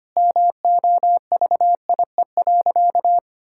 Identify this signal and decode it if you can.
Morse code, heard as a keyed tone, about 25 wpm, 700 Hz. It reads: MOVIE.
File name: practice / 25wpm / Sentences with 4-Word Phrases - 25wpm